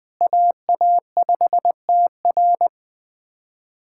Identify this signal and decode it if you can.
Morse code, heard as a keyed tone, about 20 wpm, 700 Hz. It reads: AA5TR
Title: SOTA QSO Practice 20wpm